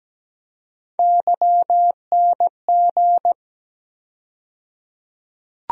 Morse code, heard as a keyed tone, about 17 wpm, 700 Hz. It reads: YNG
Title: USA Call Sign Postfixes - 3 Letters 17wpm